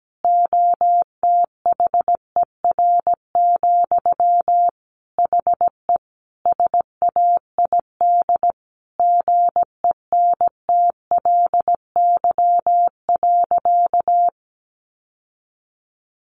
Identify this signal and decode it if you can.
Morse code, heard as a keyed tone, about 17 wpm, 700 Hz. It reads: OTHER, HE SAID GENTLY.
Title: Sentences from Top 1500 Words 17wpm